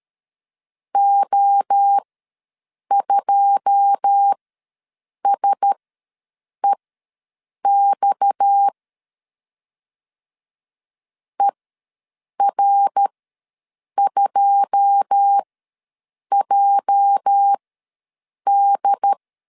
Morse code, heard as a keyed tone, about 13 wpm, spaced as for 7 wpm, 800 Hz. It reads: O2SEX ER2JD
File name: LEZIONI / 9 2 ws 12wpm 3406